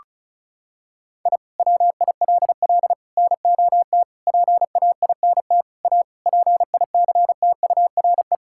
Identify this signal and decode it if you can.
Morse code, heard as a keyed tone, about 36 wpm, 700 Hz. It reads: I WILL NOT PAINT A PICTURE
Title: Sentences from Top 1000 Words - Review Part 6 - 35wpm